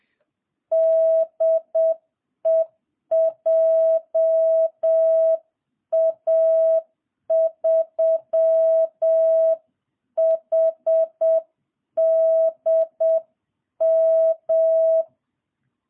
Morse code, heard as a keyed tone, about 7 wpm, 650 Hz. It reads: DE JA3HDM